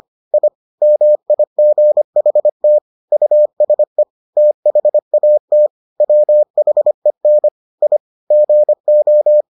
Morse code, heard as a keyed tone, about 25 wpm, 600 Hz. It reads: I MIGHT USE THAT WHEN I GO